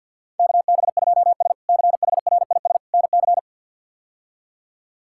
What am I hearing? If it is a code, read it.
Morse code, heard as a keyed tone, about 52 wpm, 700 Hz. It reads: KB2S CHRIS NC
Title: K1USN SST QSO Practice - 50wpm